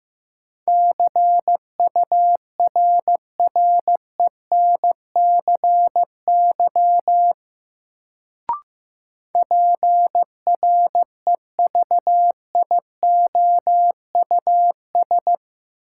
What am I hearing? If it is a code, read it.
Morse code, heard as a keyed tone, about 15 wpm, 700 Hz. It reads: CURRENCY PREVIOUS
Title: Sentences from Top 1500 Words - New Word Drill Alt - 15wpm